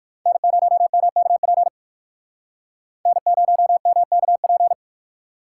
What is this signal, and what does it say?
Morse code, heard as a keyed tone, about 55 wpm, 700 Hz. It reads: N0MKP N0MKP
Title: Speed Racing - USA 1x3 Call Signs 45wpm